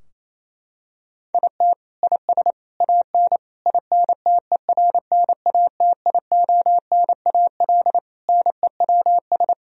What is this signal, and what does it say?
Morse code, heard as a keyed tone, about 28 wpm, 700 Hz. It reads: IT IS AN INTERNATIONAL NEWS
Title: Sentences from Top 1000 Words - Review Part 4 - 28wpm